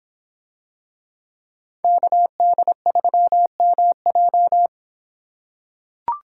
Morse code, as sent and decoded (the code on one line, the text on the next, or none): -.- -.. ...-- -- .---
KD3MJ